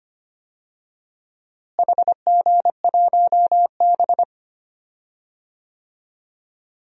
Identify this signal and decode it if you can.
Morse code, heard as a keyed tone, about 25 wpm, 700 Hz. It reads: HG1B
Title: DXCC-100 Call Signs - 4 Characters - Repeated 3x - 25wpm